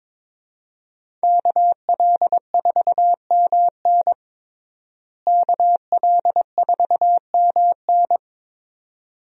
Morse code, heard as a keyed tone, about 22 wpm, 700 Hz. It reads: KL4MN KL4MN
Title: USA Call Signs - 5 Characters - Repeated 2x - 22wpm